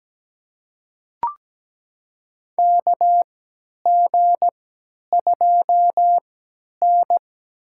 Morse code, heard as a keyed tone, about 17 wpm, spaced as for 10 wpm, 700 Hz. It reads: KG2N
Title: USA Call Signs - 4 Characters - 3x Character Spacing - 17wpm